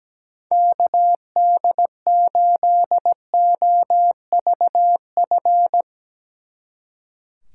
Morse code, heard as a keyed tone, about 17 wpm, 700 Hz. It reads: KD8OVF
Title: Speed Racing - USA 2x3 Call Signs 17wpm